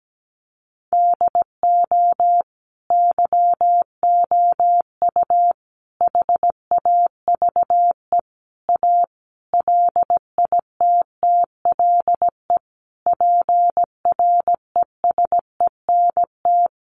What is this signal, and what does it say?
Morse code, heard as a keyed tone, about 17 wpm, 700 Hz. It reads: DO YOU HAVE A LITTLE PRESENT
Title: Sentences from Top 800 Words 17wpm